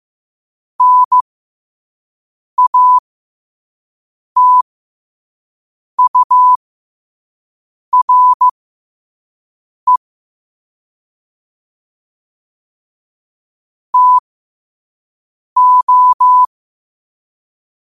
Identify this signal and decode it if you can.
Morse code, heard as a keyed tone, about 15 wpm, spaced as for 5 wpm, 1000 Hz. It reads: NATURE TO
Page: New quotes every day in morse code at 5 Words per minute.